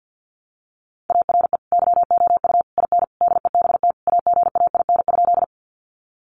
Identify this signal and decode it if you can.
Morse code, heard as a keyed tone, about 50 wpm, 700 Hz. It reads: ARE YOU IN DEBT AGAIN?
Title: Sentences from Top 1300 Words 50wpm